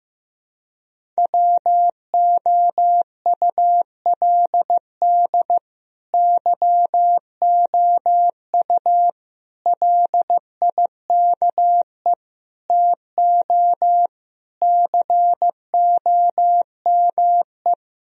Morse code, heard as a keyed tone, about 15 wpm, 700 Hz. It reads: WOULD YOU LIKE TO COME
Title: Sentences from Top 1000 Words - Review Part 1 - 15wpm